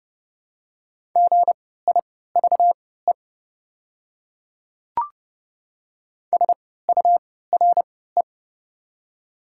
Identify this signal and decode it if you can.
Morse code, heard as a keyed tone, about 30 wpm, spaced as for 17 wpm, 700 Hz. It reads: GIVE SURE
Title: 3x Character Spacing - Top 300 Words - 30wpm